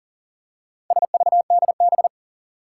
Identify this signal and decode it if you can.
Morse code, heard as a keyed tone, about 40 wpm, 700 Hz. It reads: SVDB
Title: Four Challenging Letter-Number Combinations 40wpm